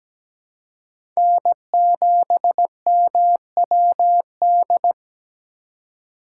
Morse code, heard as a keyed tone, about 17 wpm, 700 Hz. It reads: N7MWD
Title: USA Call Signs - 5 Characters - 4x Character Spacing - 17wpm